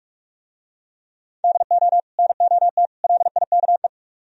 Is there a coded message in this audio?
Yes: Morse code from 1.4 to 3.9 s, heard as a keyed tone, about 47 wpm, 690 Hz. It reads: DO NOT LIKE